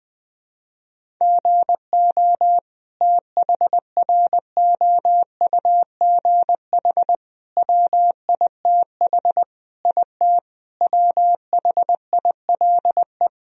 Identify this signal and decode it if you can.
Morse code, heard as a keyed tone, about 20 wpm, 700 Hz. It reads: GO THROUGH WITH IT WHILE
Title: Sentences from Top 1000 Words - Review Part 1 - 20wpm